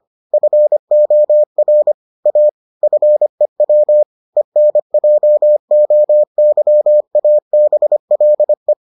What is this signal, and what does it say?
Morse code, heard as a keyed tone, about 25 wpm, 600 Hz. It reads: FOR A FEW ENJOYABLE